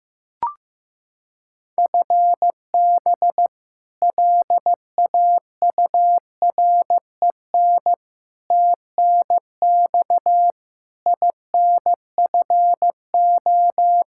Morse code, heard as a keyed tone, about 15 wpm, 700 Hz. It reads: FB LAUREN TNX INFO
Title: K1USN SST QSO Practice 15wpm